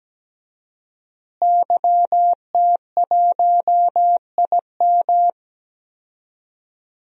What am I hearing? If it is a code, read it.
Morse code, heard as a keyed tone, about 17 wpm, 700 Hz. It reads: YT1IM
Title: DXCC-100 Call Signs - 5 Characters - Repeated 3x - 17wpm